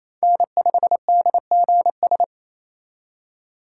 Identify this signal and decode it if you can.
Morse code, heard as a keyed tone, about 35 wpm, 700 Hz. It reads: N5DGS
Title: Speed Racing - USA 1x3 Call Signs 28wpm